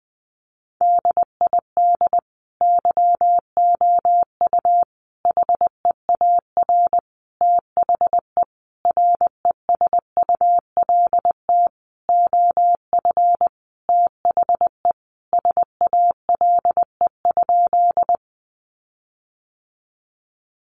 Morse code, heard as a keyed tone, about 20 wpm, 700 Hz. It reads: DID YOU HEAR THE RESULT OF THE SALE?
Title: Sentences from Top 1400 Words 20wpm